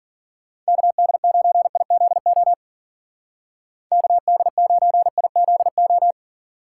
Morse code, heard as a keyed tone, about 50 wpm, 700 Hz. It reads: KD9IZO KD9IZO